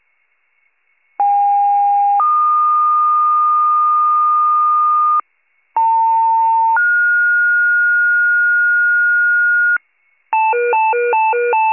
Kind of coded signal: two-tone page

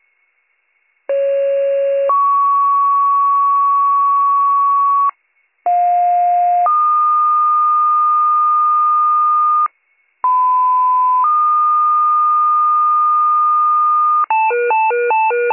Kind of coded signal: two-tone page

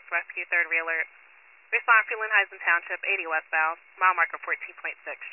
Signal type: Emergency Alert System (SAME) tones

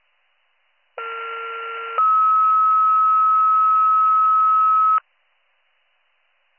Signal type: two-tone page

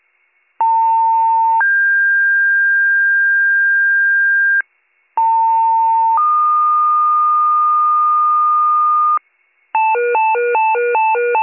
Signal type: two-tone page

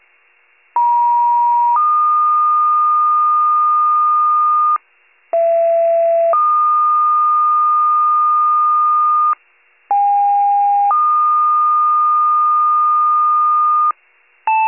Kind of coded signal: two-tone page

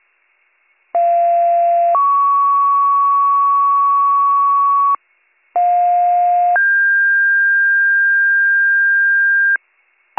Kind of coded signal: two-tone page

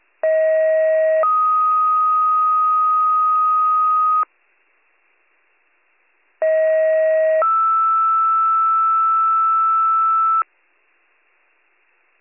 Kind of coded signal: two-tone page

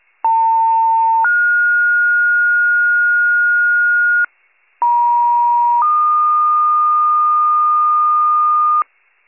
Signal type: two-tone page

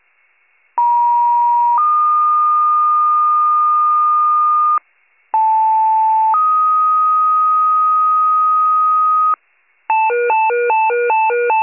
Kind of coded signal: two-tone page

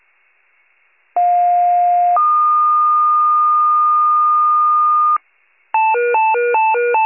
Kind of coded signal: two-tone page

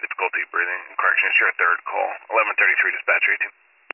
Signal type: Emergency Alert System (SAME) tones

